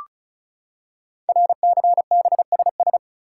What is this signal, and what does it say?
Morse code, heard as a keyed tone, about 36 wpm, 700 Hz. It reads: RCBSS
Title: Code Groups - 5 Letters - 35wpm